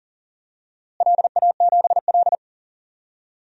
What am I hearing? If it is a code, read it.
Morse code, heard as a keyed tone, about 40 wpm, 700 Hz. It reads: LA7L
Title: DXCC-100 Call Signs - 4 Characters - Repeated 3x - 40wpm